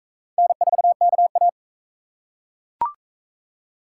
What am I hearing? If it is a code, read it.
Morse code, heard as a keyed tone, about 40 wpm, 700 Hz. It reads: N4KA